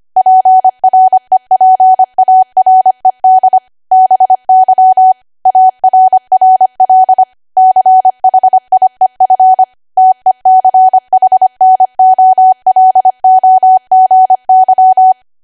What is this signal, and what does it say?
Morse code, heard as a keyed tone, about 25 wpm, 750 Hz. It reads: PREPARED BY ARRL CHIEF TECHNOLOGY